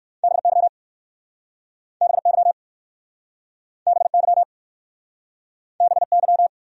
Morse code, heard as a keyed tone, about 59 wpm, 700 Hz. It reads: BY BY BY BY